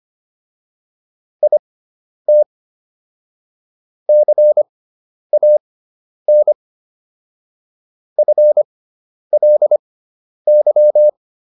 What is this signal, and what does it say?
Morse code, heard as a keyed tone, about 25 wpm, spaced as for 10 wpm, 600 Hz. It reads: IT CAN FLY